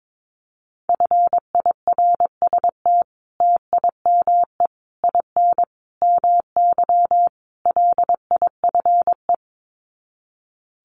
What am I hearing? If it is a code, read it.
Morse code, heard as a keyed tone, about 22 wpm, 700 Hz. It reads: FIRST TIME IN MY LIFE
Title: Sets of 5 Words Encore2 22wpm